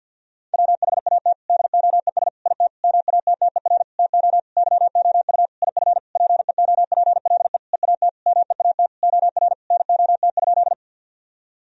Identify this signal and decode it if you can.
Morse code, heard as a keyed tone, about 50 wpm, 700 Hz. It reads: WHAT DOES IT MATTER TO YOU IF PEOPLE EAT MEAT OR NOT?
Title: Sentences from Top 1000 Words - Review Part 3 - 50wpm